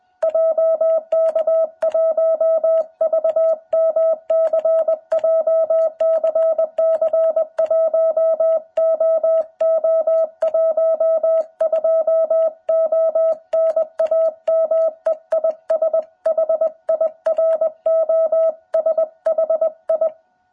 Morse code, heard as a keyed tone, about 21 wpm, 660 Hz. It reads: JK1VMC JCC1OO12O NAME IS HIROSHI
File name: jh8cbh / 2021cw